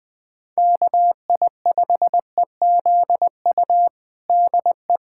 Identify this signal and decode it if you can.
Morse code, heard as a keyed tone, about 20 wpm, 700 Hz. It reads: KI5EZU DE
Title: USA Callsign DE Callsign - 6 Characters - 20wpm